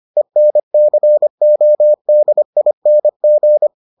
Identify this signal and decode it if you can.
Morse code, heard as a keyed tone, about 25 wpm, 600 Hz. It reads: ENCODING